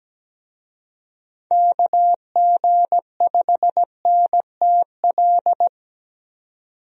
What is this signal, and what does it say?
Morse code, heard as a keyed tone, about 17 wpm, 700 Hz. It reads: KG5NTL
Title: USA Call Signs - 6 Characters - 17wpm